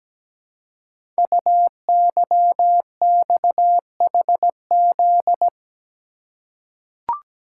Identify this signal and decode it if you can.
Morse code, heard as a keyed tone, about 17 wpm, 700 Hz. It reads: UYXHZ
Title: Code Groups - 5 Letters - 17wpm